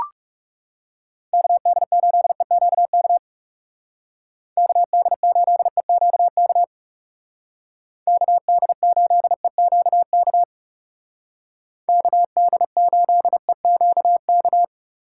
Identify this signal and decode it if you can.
Morse code, heard as a keyed tone, about 39 wpm, 700 Hz. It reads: KD8EQK KD8EQK KD8EQK KD8EQK